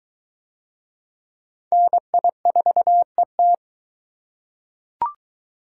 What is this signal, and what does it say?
Morse code, heard as a keyed tone, about 22 wpm, 700 Hz. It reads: NI4ET